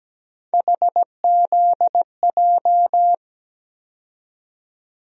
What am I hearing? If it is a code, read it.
Morse code, heard as a keyed tone, about 17 wpm, 700 Hz. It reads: HZJ
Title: USA Call Sign Postfixes - 3 Letters 17wpm